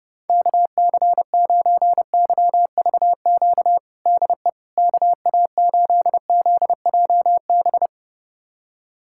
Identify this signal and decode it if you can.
Morse code, heard as a keyed tone, about 29 wpm, 700 Hz. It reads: KC9YVQ DE KA8ZJB
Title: USA Callsign DE Callsign - 6 Characters - 30wpm